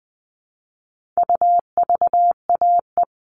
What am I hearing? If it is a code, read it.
Morse code, heard as a keyed tone, about 20 wpm, 700 Hz. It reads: UVAE
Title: All Cut Numbers - 4-Digit 20wpm